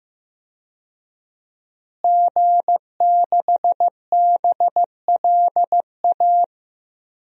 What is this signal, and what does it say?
Morse code, heard as a keyed tone, about 15 wpm, 700 Hz. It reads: G6BLA
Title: DXCC-100 Call Signs - 5 Characters - Repeated 2x - 15wpm